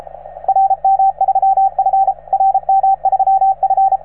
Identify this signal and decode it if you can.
Morse code, heard as a keyed tone, about 33 wpm, 770 Hz, with radio noise. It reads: RM3F RM3F